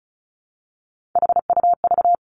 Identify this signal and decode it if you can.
Morse code, heard as a keyed tone, about 36 wpm, 700 Hz. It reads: HUV
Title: Three Challenging Letter-Number Combinations 35wpm